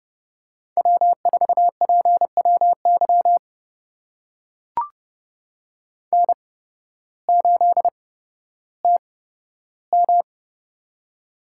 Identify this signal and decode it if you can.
Morse code, heard as a keyed tone, about 31 wpm, spaced as for 14 wpm, 700 Hz. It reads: W4PWY N8TM
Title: USA Call Signs - 8x Character Spacing - 30wpm